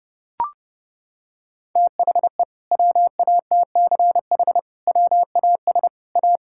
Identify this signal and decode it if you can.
Morse code, heard as a keyed tone, about 30 wpm, 700 Hz. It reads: THE WATCH WAS A
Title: Sentences from Top 1000 Words - Review Part 6 - 30wpm